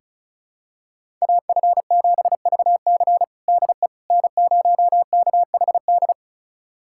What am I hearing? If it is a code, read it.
Morse code, heard as a keyed tone, about 36 wpm, 700 Hz. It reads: AF7VC DE N0KHD
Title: USA Callsign DE Callsign - 5 Characters - 35wpm